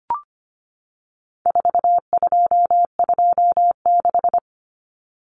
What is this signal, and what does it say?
Morse code, heard as a keyed tone, about 25 wpm, 700 Hz. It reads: 4226
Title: Common Cut Numbers - 4-Digit 25wpm